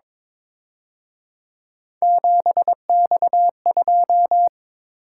Morse code, heard as a keyed tone, about 26 wpm, 700 Hz. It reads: 7X2
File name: Speed-Racing - Three Letter-Number Combinations - 22wpm